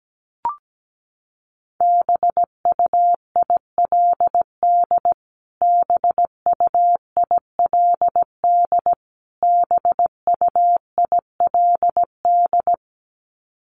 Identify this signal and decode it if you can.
Morse code, heard as a keyed tone, about 17 wpm, 700 Hz. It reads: BUILD BUILD BUILD
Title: Top 500 Words Repeated 17wpm